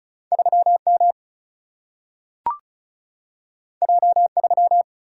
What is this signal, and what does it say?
Morse code, heard as a keyed tone, about 36 wpm, 700 Hz. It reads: 3M J3
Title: Single-Letter Single-Number 35wpm